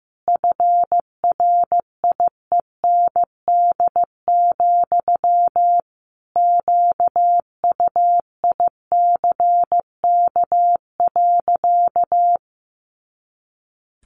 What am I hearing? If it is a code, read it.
Morse code, heard as a keyed tone, about 15 wpm, 700 Hz. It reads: FRIEND, QUICK.
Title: Sentences from Top 500 Words 15wpm